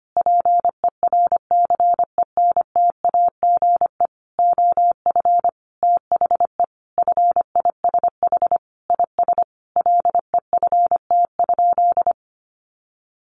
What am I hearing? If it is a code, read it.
Morse code, heard as a keyed tone, about 25 wpm, 700 Hz. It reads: PERCENTAGE OF THE FISH IS LEFT?
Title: Sentences from Top 1300 Words 25wpm